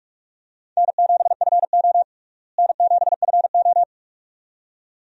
Morse code, heard as a keyed tone, about 42 wpm, 700 Hz. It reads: N7FO N7FO
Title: USA Call Signs - 4 Characters - Repeated 2x - 45wpm